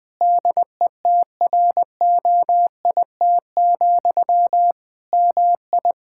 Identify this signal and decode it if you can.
Morse code, heard as a keyed tone, about 21 wpm, 700 Hz. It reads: DETROIT, MI
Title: Practice QSO - Exchange 1 - Beginning - 20wpm